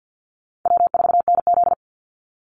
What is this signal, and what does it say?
Morse code, heard as a keyed tone, about 50 wpm, 700 Hz. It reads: W4NZ
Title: Call Signs Repeated 3x - 50wpm